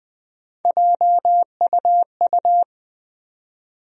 Morse code, heard as a keyed tone, about 20 wpm, 700 Hz. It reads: JUU
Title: USA Call Sign Postfixes - 3 Letters 20wpm